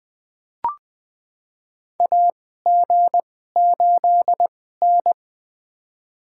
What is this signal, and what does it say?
Morse code, heard as a keyed tone, about 20 wpm, spaced as for 14 wpm, 700 Hz. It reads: AG8N
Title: USA Call Signs - 4 Characters - 2x Character Spacing - 20wpm